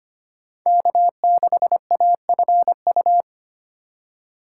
Morse code, heard as a keyed tone, about 25 wpm, 700 Hz. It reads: K6AFU
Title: USA Call Signs - 5 Characters - 2x Character Spacing - 25wpm